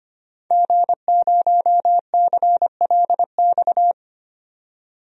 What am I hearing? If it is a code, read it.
Morse code, heard as a keyed tone, about 25 wpm, 700 Hz. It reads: G0CLX
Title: DXCC-100 Call Signs - 5 Characters - 25wpm